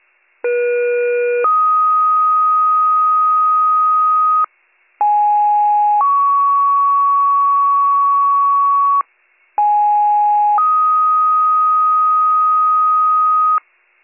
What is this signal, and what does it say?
Two-tone page, heard as a one-second tone, then a three-second tone, about 480 then 1250 Hz, about 830 then 1120 Hz, about 830 then 1250 Hz. It is heard from 0.4 to 13.6 s.